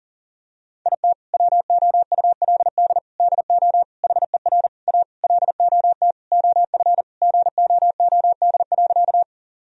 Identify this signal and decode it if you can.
Morse code, heard as a keyed tone, about 40 wpm, 700 Hz. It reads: IT WOULD DO HER A LOT OF GOOD.
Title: Sentences from Top 1000 Words - Review Part 3 - 40wpm